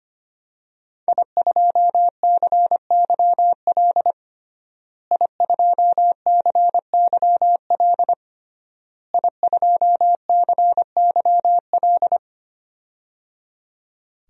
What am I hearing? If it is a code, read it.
Morse code, heard as a keyed tone, about 25 wpm, 700 Hz. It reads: I2CYL I2CYL I2CYL
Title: DXCC-100 Call Signs - 5 Characters - Repeated 3x - 25wpm